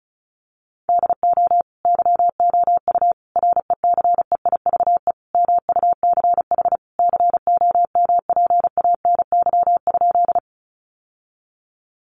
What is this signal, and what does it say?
Morse code, heard as a keyed tone, about 34 wpm, 700 Hz. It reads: DO YOU RECEIVE MUCH COMPANY?
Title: Sentences from Top 1000 Words 35wpm